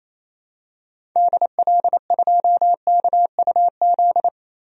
Morse code, heard as a keyed tone, about 28 wpm, 700 Hz. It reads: DL2KUZ
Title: DXCC-100 Call Signs - 6 Characters - 28wpm